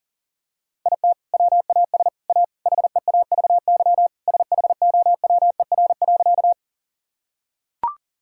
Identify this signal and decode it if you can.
Morse code, heard as a keyed tone, about 40 wpm, 700 Hz. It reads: IT WAS A HEAVY SHOWER.
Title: Sentences from Top 1600 Words 40wpm